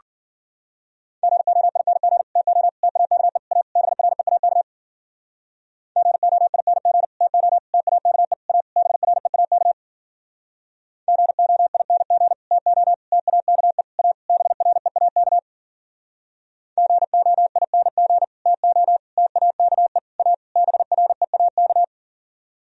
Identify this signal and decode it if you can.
Morse code, heard as a keyed tone, about 54 wpm, 700 Hz. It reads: GOING TO TAKE A BREAK GOING TO TAKE A BREAK GOING TO TAKE A BREAK GOING TO TAKE A BREAK